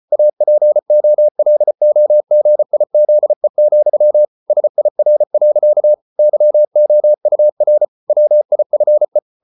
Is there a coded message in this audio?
Yes: Morse code from 0.1 to 9.2 s, heard as a keyed tone, about 34 wpm, 600 Hz. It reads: APOLOGIZE, SIR. YOUR WIFE